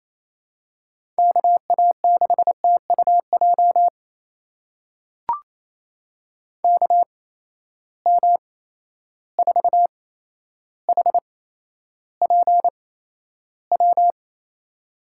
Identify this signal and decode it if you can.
Morse code, heard as a keyed tone, about 28 wpm, spaced as for 12 wpm, 700 Hz. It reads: KA6TUJ KM4HPW
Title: USA Call Signs - 6 Characters - 8x Character Spacing - 28wpm